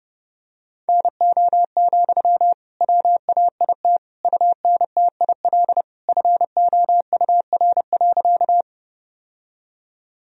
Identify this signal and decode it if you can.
Morse code, heard as a keyed tone, about 30 wpm, 700 Hz. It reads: NO, WAIT UNTIL FOUR.
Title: Sentences from Top 1000 Words - Review Part 1 - 30wpm